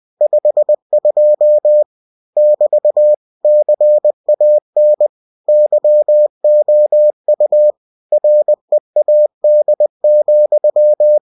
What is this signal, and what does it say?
Morse code, heard as a keyed tone, about 20 wpm, 600 Hz. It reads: 52 <BT> CAN YOU READ,
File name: cw-books / winnie-the-pooh / winnie-20wpm-page-0052